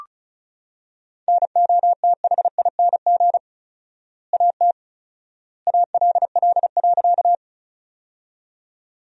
Morse code, heard as a keyed tone, about 34 wpm, 700 Hz. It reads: NOTHING AT ALL.
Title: 4x Word-Spacing - Sentences from Top 500 Words - 35wpm